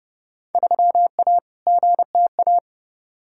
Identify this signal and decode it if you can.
Morse code, heard as a keyed tone, about 30 wpm, 700 Hz. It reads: 3A GTA
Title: Field Day Exchanges - 30wpm